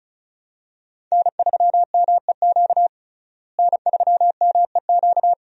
Morse code, heard as a keyed tone, about 34 wpm, 700 Hz. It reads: N3MEQ N3MEQ
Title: USA Call Signs - 5 Characters - Repeated 2x - 35wpm